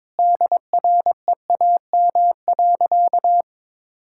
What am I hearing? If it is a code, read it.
Morse code, heard as a keyed tone, about 22 wpm, 700 Hz. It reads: DREAM.
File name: practice / 22wpm / Sentences with 2-Word Phrases - 22wpm